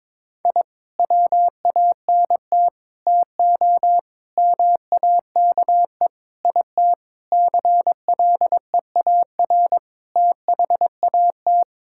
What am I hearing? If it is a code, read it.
Morse code, heard as a keyed tone, about 21 wpm, 700 Hz. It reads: I WANT TO MAKE IT CLEAR THAT
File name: practice / 22wpm / Sets of 7 Words - 22wpm